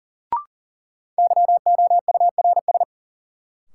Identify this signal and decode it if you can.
Morse code, heard as a keyed tone, about 40 wpm, 700 Hz. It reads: YOURS
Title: Sentences from Top 1500 Words - New Word Drill Alt - 40wpm